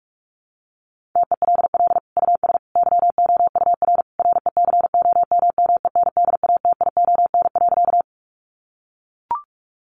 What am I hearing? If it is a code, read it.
Morse code, heard as a keyed tone, about 44 wpm, 700 Hz. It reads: TELL US YOUR RECOMMENDATION.
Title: Sentences from Top 1400 Words 45wpm